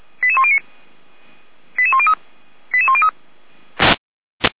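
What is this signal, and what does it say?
Five-tone selective call (ZVEI), heard as quick runs of five tones, tones of about 70 ms. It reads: 88109 88103 88103
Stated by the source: ZVEI five-tone selective call received on an SDR.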